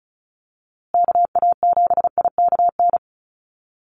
Morse code, heard as a keyed tone, about 34 wpm, 700 Hz. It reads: KA7IKN
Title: USA Call Signs Repeated 2x 35wpm